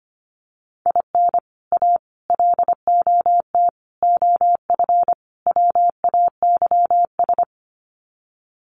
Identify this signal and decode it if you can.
Morse code, heard as a keyed tone, about 25 wpm, 700 Hz. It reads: IN A LOT OF WAYS
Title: Sets of 5 Words Encore 25wpm